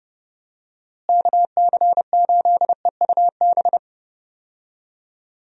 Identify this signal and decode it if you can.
Morse code, heard as a keyed tone, about 31 wpm, 700 Hz. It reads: KC8EUB
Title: USA Call Signs - 6 Characters - Repeated 2x - 30wpm